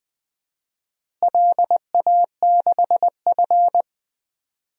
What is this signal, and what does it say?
Morse code, heard as a keyed tone, about 20 wpm, 700 Hz. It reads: LA6F
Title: DXCC-100 Call Signs - 4 Characters - Repeated 3x - 20wpm